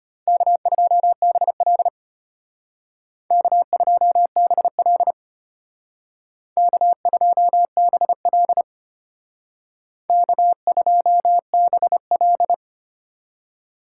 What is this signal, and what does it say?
Morse code, heard as a keyed tone, about 32 wpm, 700 Hz. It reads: K2BL K2BL K2BL K2BL